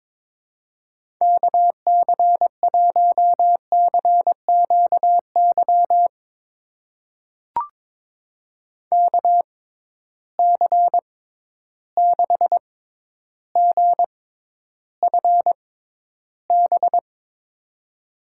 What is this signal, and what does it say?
Morse code, heard as a keyed tone, about 22 wpm, spaced as for 11 wpm, 700 Hz. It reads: KC1CQY KC6GFB
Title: USA Call Signs - 6x Character Spacing - 22wpm